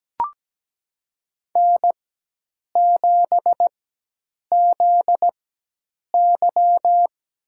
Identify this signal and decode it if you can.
Morse code, heard as a keyed tone, about 17 wpm, spaced as for 8 wpm, 700 Hz. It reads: N7ZY
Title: USA Call Signs - 4 Characters - 4x Character Spacing - 17wpm